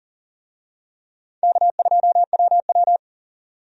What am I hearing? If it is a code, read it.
Morse code, heard as a keyed tone, about 40 wpm, 700 Hz. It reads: K2WW
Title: USA Call Signs - 4 Characters - 40wpm